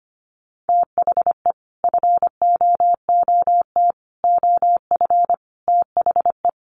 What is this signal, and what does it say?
Morse code, heard as a keyed tone, about 25 wpm, 700 Hz. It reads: THE FOOT OF THE
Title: Sets of 5 Words Encore2 25wpm